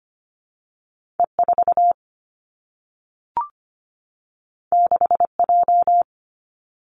Morse code, heard as a keyed tone, about 25 wpm, 700 Hz. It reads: E4 6J
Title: Two Challenging Letter-Number Combinations 25wpm